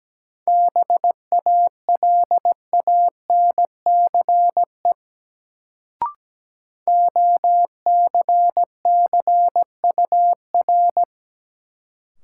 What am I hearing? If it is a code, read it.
Morse code, heard as a keyed tone, about 17 wpm, 700 Hz. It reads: BALANCE OCCUR
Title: Sentences from Top 900 Words - New Word Drill - 17wpm